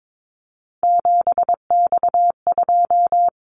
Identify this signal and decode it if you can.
Morse code, heard as a keyed tone, about 22 wpm, 700 Hz. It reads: 7X2
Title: Callsign Prefixes for Top 100 DXCC - 22wpm